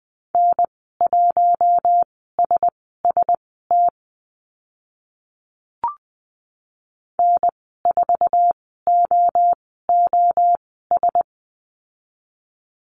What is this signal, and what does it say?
Morse code, heard as a keyed tone, about 20 wpm, spaced as for 14 wpm, 700 Hz. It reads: N1SST N4OOS
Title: 062 Lesson 11g - Letter h - Call Signs 20wpm